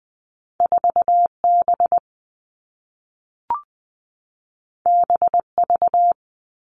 Morse code, heard as a keyed tone, about 20 wpm, 700 Hz. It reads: 4B BV
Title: All Cut Numbers - 2-Digit 20wpm